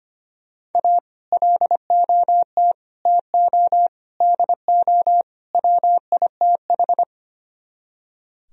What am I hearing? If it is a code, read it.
Morse code, heard as a keyed tone, about 33 wpm, 700 Hz. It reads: A LOT TO DO WITH